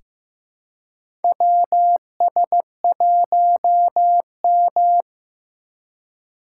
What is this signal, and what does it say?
Morse code, heard as a keyed tone, about 15 wpm, 700 Hz. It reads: WS1M
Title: USA Call Signs - 4 Characters - 4x Character Spacing - 15wpm